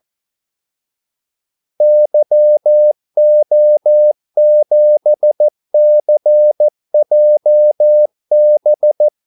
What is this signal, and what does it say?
Morse code, heard as a keyed tone, about 14 wpm, 600 Hz. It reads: YO7CJB
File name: Callsigns-14wpm